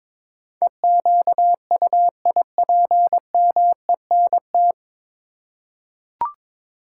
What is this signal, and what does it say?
Morse code, heard as a keyed tone, about 22 wpm, 700 Hz. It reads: EQUIPMENT
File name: practice / 22wpm / 9 Letter Words - First 3 Letters Given - 22wpm